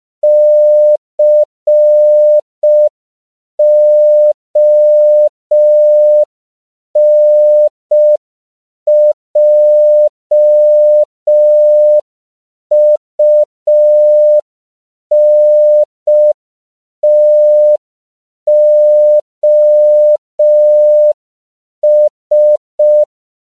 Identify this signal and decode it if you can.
Morse code, heard as a keyed tone, about 5 wpm, 600 Hz. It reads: CONJUNTOS
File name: CW / 5PPM